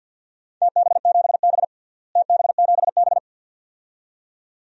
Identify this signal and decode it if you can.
Morse code, heard as a keyed tone, about 48 wpm, 700 Hz. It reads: TB7B TB7B
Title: DXCC-100 Call Signs - 4 Characters - Repeated 2x - 50wpm